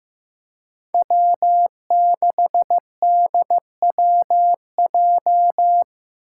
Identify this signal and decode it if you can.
Morse code, heard as a keyed tone, about 15 wpm, 700 Hz. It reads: W6DWJ
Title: Five Challenging Letter-Number Combinations 15wpm